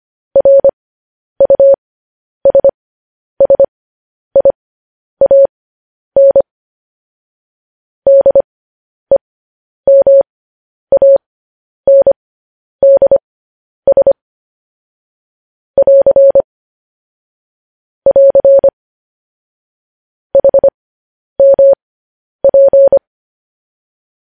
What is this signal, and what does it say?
Morse code, heard as a keyed tone, about 25 wpm, spaced as for 10 wpm, 550 Hz. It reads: RUSSIAN DEMANDS <AR> <AR> HMP
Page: news-headlines-in-morse-code-at-10-wpm